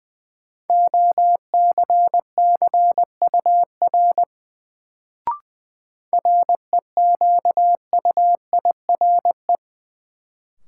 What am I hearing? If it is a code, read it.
Morse code, heard as a keyed tone, about 20 wpm, 700 Hz. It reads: OCCUR REQUIRE
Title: Sentences from Top 900 Words - New Word Drill - 20wpm